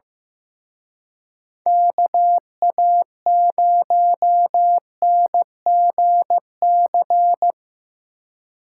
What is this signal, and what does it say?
Morse code, heard as a keyed tone, about 18 wpm, 700 Hz. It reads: KA0NGC
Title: Speed Racing - USA Call Signs 15wpm